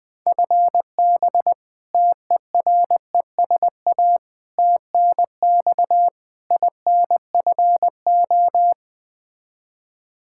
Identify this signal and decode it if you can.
Morse code, heard as a keyed tone, about 21 wpm, 700 Hz. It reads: FB TERESA TNX INFO
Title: Practice QSO - Exchange 3 - More Info - 20wpm